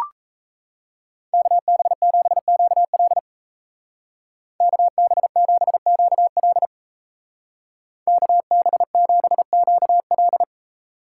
Morse code, heard as a keyed tone, about 33 wpm, 700 Hz. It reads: KB7QL KB7QL KB7QL
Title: Speed Racing - USA 2x2 Call Signs 28wpm